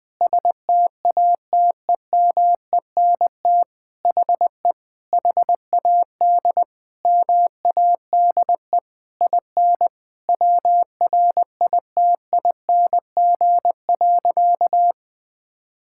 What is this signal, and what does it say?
Morse code, heard as a keyed tone, about 20 wpm, 700 Hz. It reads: STATEMENT HE HAD MADE IN WRITING.
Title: Sentences from Top 1600 Words 20wpm